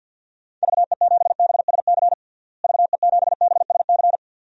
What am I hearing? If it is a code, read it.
Morse code, heard as a keyed tone, about 53 wpm, 700 Hz. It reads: VE7BSC VE7BSC
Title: DXCC-100 Call Signs - 6 Characters - Repeated 2x - 50wpm